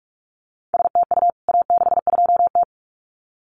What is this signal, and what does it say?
Morse code, heard as a keyed tone, about 44 wpm, 700 Hz. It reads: STU A62T
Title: CWT Exchange 45wpm